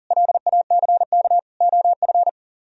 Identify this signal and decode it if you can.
Morse code, heard as a keyed tone, about 48 wpm, 700 Hz. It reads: LACK OF
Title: Speed-Racing - Sets of 3 Words 40wpm